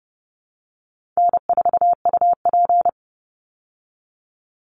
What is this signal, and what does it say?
Morse code, heard as a keyed tone, about 30 wpm, 700 Hz. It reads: N4UP
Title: Easy Call Signs Repeated 3x - 30wpm